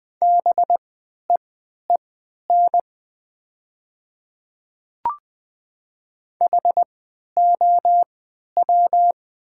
Morse code, heard as a keyed tone, about 20 wpm, spaced as for 11 wpm, 690 Hz. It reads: BEEN HOW
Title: 3x Character Spacing - Top 300 Words - 20wpm